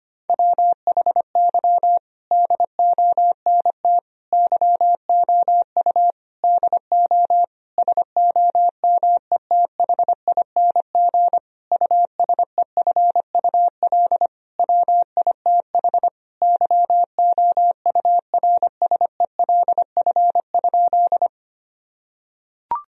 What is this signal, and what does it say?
Morse code, heard as a keyed tone, about 25 wpm, 700 Hz. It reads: WHY DONT YOU DO SOMETHING USEFUL WITH YOURSELF?
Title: Sentences from Top 1000 Words - Review Part 4 - 25wpm